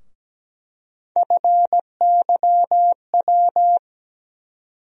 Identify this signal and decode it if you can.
Morse code, heard as a keyed tone, about 17 wpm, 700 Hz. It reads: FYW
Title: USA Call Sign Postfixes - 3 Letters 17wpm